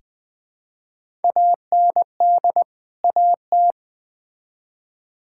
Morse code, heard as a keyed tone, about 20 wpm, 700 Hz. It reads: AND AT
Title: Build A Phrase - 5-Word Phrases - 20wpm